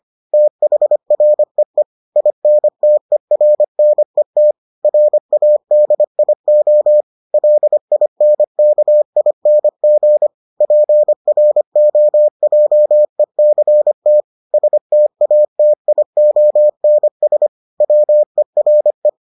Morse code, heard as a keyed tone, about 25 wpm, 600 Hz. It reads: THREE INTERNET RADIO LINKING PROJECT STATIONS WERE